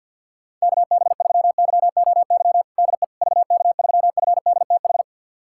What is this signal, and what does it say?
Morse code, heard as a keyed tone, about 51 wpm, 700 Hz. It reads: KB3YOY DE VK3FDTH
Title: USA Callsign DE Callsign 50wpm